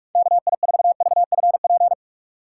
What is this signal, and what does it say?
Morse code, heard as a keyed tone, about 46 wpm, 700 Hz. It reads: KI4VFP
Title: USA Call Signs - 6 Characters - 3x Character Spacing - 45wpm